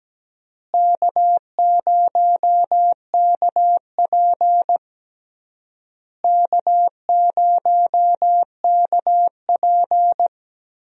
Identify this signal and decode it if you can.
Morse code, heard as a keyed tone, about 17 wpm, 700 Hz. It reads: K0KP K0KP
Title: USA Call Signs - 4 Characters - Repeated 3x - 17wpm